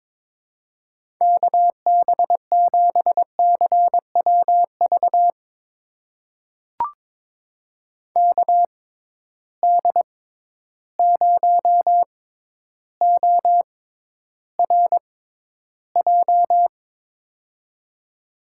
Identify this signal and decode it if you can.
Morse code, heard as a keyed tone, about 22 wpm, spaced as for 11 wpm, 700 Hz. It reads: KB7CWV KD0ORJ
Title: USA Call Signs - 6 Characters - 6x Character Spacing - 22wpm